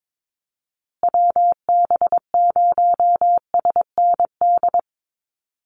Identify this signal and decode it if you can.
Morse code, heard as a keyed tone, about 22 wpm, 700 Hz. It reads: WB0SND
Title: Call Signs Repeated 3x - 22wpm